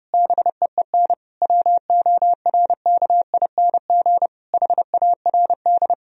Morse code, heard as a keyed tone, about 30 wpm, 700 Hz. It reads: BEEN WORKING HARD
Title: Sentences from Top 1000 Words - Review Part 4 - 30wpm